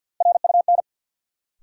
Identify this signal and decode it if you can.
Morse code, heard as a keyed tone, about 53 wpm, 700 Hz. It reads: RUN
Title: Sets of 4 Words - New Word Drill - 50wpm